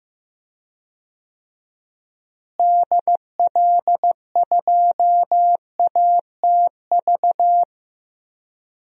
Morse code, heard as a keyed tone, about 15 wpm, 700 Hz. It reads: DL2ATV
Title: DXCC-100 Call Signs - 6 Characters - Repeated 3x - 15wpm